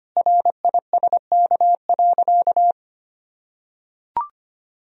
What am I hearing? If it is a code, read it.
Morse code, heard as a keyed tone, about 25 wpm, 700 Hz. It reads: RISK.
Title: Sentences from Top 1000 Words - Review Part 4 - 25wpm